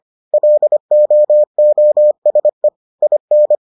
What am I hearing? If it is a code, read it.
Morse code, heard as a keyed tone, about 25 wpm, 600 Hz. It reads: LOOSE IN